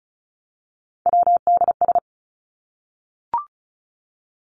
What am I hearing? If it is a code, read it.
Morse code, heard as a keyed tone, about 34 wpm, 700 Hz. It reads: WDS
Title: Abbreviations Repeated 3x - 35wpm